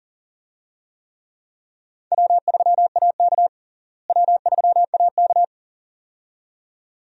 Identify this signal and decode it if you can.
Morse code, heard as a keyed tone, about 40 wpm, 700 Hz. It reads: W3AK W3AK
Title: USA Call Signs - 4 Characters - Repeated 2x - 40wpm